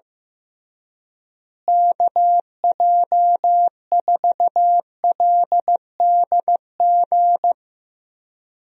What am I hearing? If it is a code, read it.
Morse code, heard as a keyed tone, about 15 wpm, 700 Hz. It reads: KJ4LDG
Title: USA Call Signs - 6 Characters - Repeated 3x - 15wpm